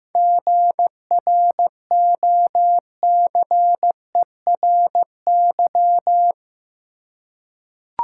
Morse code, heard as a keyed tone, about 16 wpm, 700 Hz. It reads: GROCERY
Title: Seven Letter Words - 15wpm